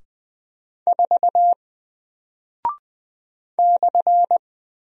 Morse code, heard as a keyed tone, about 20 wpm, 700 Hz. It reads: 4 /
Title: Single Character (Letter:Number:Punctuation:Prosign) Alt 20wpm